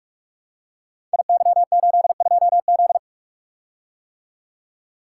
Morse code, heard as a keyed tone, about 42 wpm, 700 Hz. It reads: IY82Z
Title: Code Groups - 5 Letter:Numbers - 45wpm